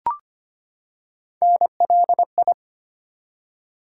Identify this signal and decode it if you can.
Morse code, heard as a keyed tone, about 25 wpm, 700 Hz. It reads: NLI
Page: morse-code-ninja-practice-25wpm